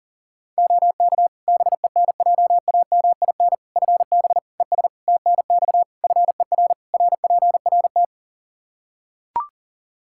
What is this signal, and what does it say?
Morse code, heard as a keyed tone, about 40 wpm, 700 Hz. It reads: OK BENJAMIN FB ES TNX FER RPRT
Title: K1USN SST QSO Practice 40wpm